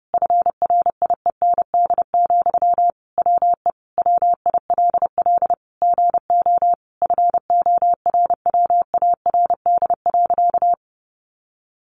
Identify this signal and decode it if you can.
Morse code, heard as a keyed tone, about 30 wpm, 700 Hz. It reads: FRIEND, WE WILL GO FORWARD.
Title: Sentences from Top 1300 Words 30wpm